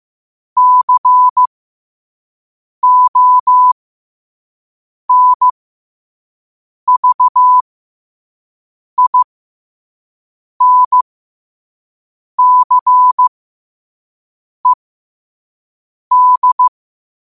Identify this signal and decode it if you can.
Morse code, heard as a keyed tone, about 15 wpm, spaced as for 5 wpm, 1000 Hz. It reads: CONVINCED
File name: qotdfiles / 5wpm